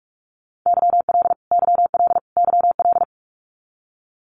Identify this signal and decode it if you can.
Morse code, heard as a keyed tone, about 44 wpm, 700 Hz. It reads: YL YL YL
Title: Abbreviations Repeated 3x - 45wpm